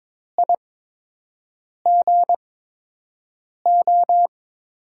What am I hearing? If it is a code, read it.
Morse code, heard as a keyed tone, about 22 wpm, spaced as for 6 wpm, 700 Hz. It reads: IGO
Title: USA Call Signs - 8x Character Spacing - 22wpm